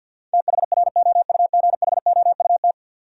Morse code, heard as a keyed tone, about 51 wpm, 700 Hz. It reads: THROUGHOUT
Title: Sentences from Top 800 Words - New Word Drill - 50wpm